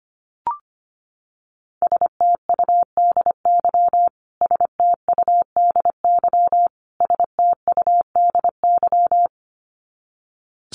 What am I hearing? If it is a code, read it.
Morse code, heard as a keyed tone, about 25 wpm, 700 Hz. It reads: STUDY STUDY STUDY
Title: Top 500 Words Repeated 25wpm